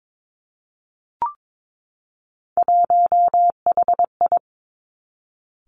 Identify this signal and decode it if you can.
Morse code, heard as a keyed tone, about 22 wpm, 700 Hz. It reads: 1HI
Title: Three Challenging Letter-Number Combinations 22wpm